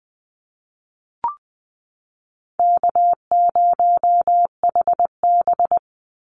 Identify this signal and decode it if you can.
Morse code, heard as a keyed tone, about 20 wpm, 700 Hz. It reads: K0HB
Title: Easy Call Signs - 20wpm